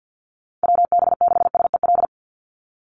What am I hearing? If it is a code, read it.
Morse code, heard as a keyed tone, about 50 wpm, 700 Hz. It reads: WB6SEL
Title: Call Signs Repeated 3x - 50wpm